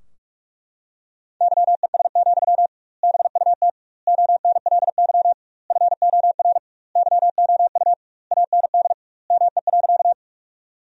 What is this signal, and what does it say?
Morse code, heard as a keyed tone, about 41 wpm, 700 Hz. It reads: YES, BUT ONLY FOR YOU AND ME.
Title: 2x Word-Spacing - Sentences from Top 500 Words - 45wpm